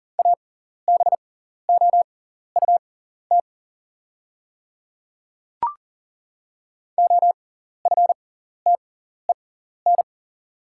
Morse code, heard as a keyed tone, about 40 wpm, spaced as for 14 wpm, 700 Hz. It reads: ABOUT OFTEN
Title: 6x Character Spacing - Top 300 Words - 40wpm